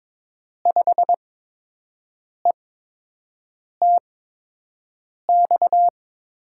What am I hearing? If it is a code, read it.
Morse code, heard as a keyed tone, about 22 wpm, spaced as for 6 wpm, 700 Hz. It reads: 5ETX
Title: USA Call Signs - 6 Characters - 8x Character Spacing - 22wpm